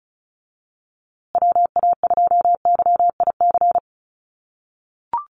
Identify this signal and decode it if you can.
Morse code, heard as a keyed tone, about 34 wpm, 700 Hz. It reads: WA2YIC
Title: USA Call Signs Repeated 3x 35wpm